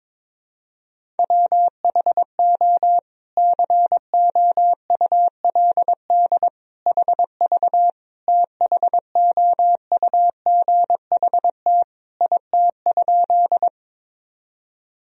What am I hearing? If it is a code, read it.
Morse code, heard as a keyed tone, about 22 wpm, 700 Hz. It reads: WHO COULD HV THOUGHT IT?
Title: Sentences from Top 300 Words and Abbreviations 22wpm